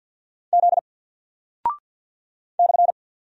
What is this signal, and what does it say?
Morse code, heard as a keyed tone, about 48 wpm, 700 Hz. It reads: Z /
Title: Single Character (Letter:Number:Punctuation:Prosign) Alt 50wpm